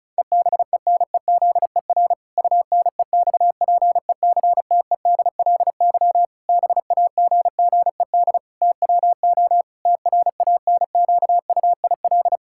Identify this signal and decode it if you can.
Morse code, heard as a keyed tone, about 36 wpm, 700 Hz. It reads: EBENEZER UNEXPECTEDLY BAGGED TWO TRANQUIL
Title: Pangram Challenge v2 - 35wpm